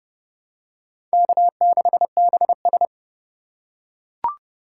Morse code, heard as a keyed tone, about 32 wpm, 700 Hz. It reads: K6BS
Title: USA Call Signs - 4 Characters - 6x Character Spacing - 30wpm